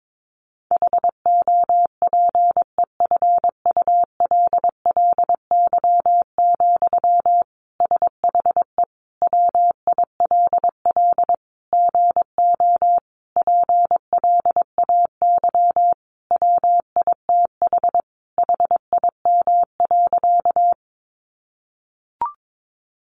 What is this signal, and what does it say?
Morse code, heard as a keyed tone, about 22 wpm, 700 Hz. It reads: HOPEFULLY, SHE WILL GO PLAY WITH HIM.
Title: Sentences from Top 1500 Words 22wpm